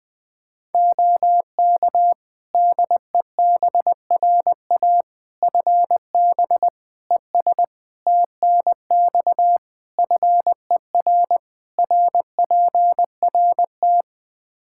Morse code, heard as a keyed tone, about 21 wpm, 700 Hz. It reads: OK DEBRA FB ES TNX FER RPRT
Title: Practice QSO - Exchange 2 - Info - 20wpm